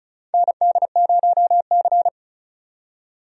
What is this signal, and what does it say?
Morse code, heard as a keyed tone, about 34 wpm, 700 Hz. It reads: ND0C
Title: USA Call Signs - 4 Characters - 35wpm